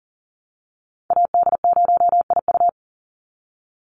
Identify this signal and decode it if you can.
Morse code, heard as a keyed tone, about 40 wpm, 700 Hz. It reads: AD0IU
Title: Call Signs - 40wpm